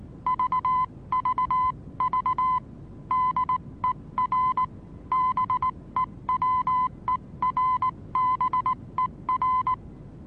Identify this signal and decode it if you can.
Morse code, heard as a keyed tone, about 19 wpm, 1010 Hz. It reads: VVV DER BEWERBER
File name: repos / Morsekurs / DJ4UF Morsen26